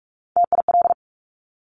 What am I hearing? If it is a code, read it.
Morse code, heard as a keyed tone, about 46 wpm, 700 Hz. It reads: TIL
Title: Top Three Letter Combinations in Words 45wpm